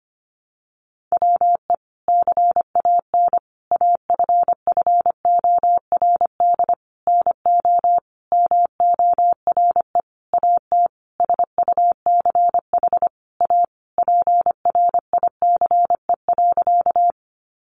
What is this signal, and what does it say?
Morse code, heard as a keyed tone, about 25 wpm, 700 Hz. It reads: WE CAN AFFORD NO MORE AT SUCH A PRICE.
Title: Sentences from Top 1100 Words 25wpm